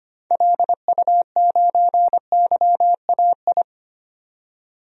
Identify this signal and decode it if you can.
Morse code, heard as a keyed tone, about 25 wpm, 700 Hz. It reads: LU9YAI
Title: DXCC-100 Call Signs - 6 Characters - 25wpm